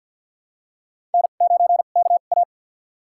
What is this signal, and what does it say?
Morse code, heard as a keyed tone, about 47 wpm, spaced as for 35 wpm, 700 Hz. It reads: N9KA
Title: USA Call Signs - 4 Characters - 2x Character Spacing - 50wpm